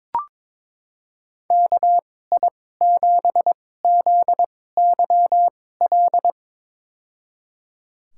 Morse code, heard as a keyed tone, about 22 wpm, spaced as for 16 wpm, 700 Hz. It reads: KI7ZYL
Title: USA Call Signs - 6 Characters - 2x Character Spacing - 22wpm